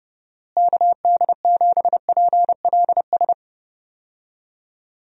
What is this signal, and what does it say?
Morse code, heard as a keyed tone, about 31 wpm, 700 Hz. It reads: KD7PLS
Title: USA Call Signs - 6 Characters - Repeated 2x - 30wpm